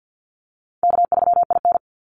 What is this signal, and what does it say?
Morse code, heard as a keyed tone, about 50 wpm, 700 Hz. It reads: K3IN
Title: Call Signs - 50wpm